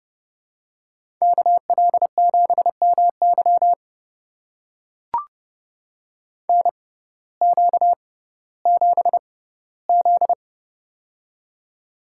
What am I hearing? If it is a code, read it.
Morse code, heard as a keyed tone, about 30 wpm, spaced as for 17 wpm, 700 Hz. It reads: KL7MY NQ7Z
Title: USA Call Signs - 6x Character Spacing - 30wpm